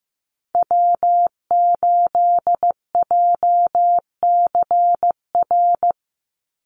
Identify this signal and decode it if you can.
Morse code, heard as a keyed tone, about 15 wpm, 700 Hz. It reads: W8JCR
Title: USA Call Signs 15wpm